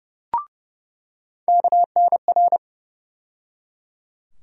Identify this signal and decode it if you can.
Morse code, heard as a keyed tone, about 31 wpm, 700 Hz. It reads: KNR
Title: USA Call Sign Postfixes - 3 Letters 30wpm